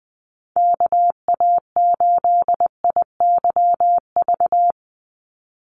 Morse code, heard as a keyed tone, about 20 wpm, 700 Hz. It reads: KA8IYV
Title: USA Call Signs 20wpm